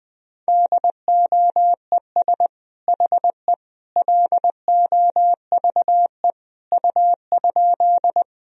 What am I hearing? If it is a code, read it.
Morse code, heard as a keyed tone, about 20 wpm, 700 Hz. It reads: DOES HE LOVE U?
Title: Sentences from Top 400 Words and Abbreviations 20wpm